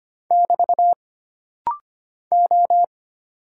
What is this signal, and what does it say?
Morse code, heard as a keyed tone, about 25 wpm, 700 Hz. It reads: <BT> O
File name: Single Character (Letter:Number:Punctuation:Prosign) - Rapid-Fire - 25wpm